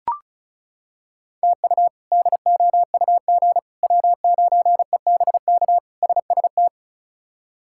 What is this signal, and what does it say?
Morse code, heard as a keyed tone, about 34 wpm, 700 Hz. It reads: TU DOUG W9EBK SST
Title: K1USN SST QSO Practice - 35wpm